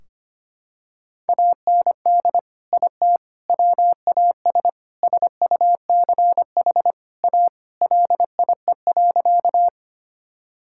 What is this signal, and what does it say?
Morse code, heard as a keyed tone, about 25 wpm, 700 Hz. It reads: AND IT WAS SUCH A LIE.
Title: Sentences from Top 1600 Words 25wpm